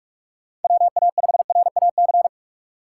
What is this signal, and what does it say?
Morse code, heard as a keyed tone, about 42 wpm, 700 Hz. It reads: WA5RAC
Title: USA Call Signs - 6 Characters - 4x Character Spacing - 45wpm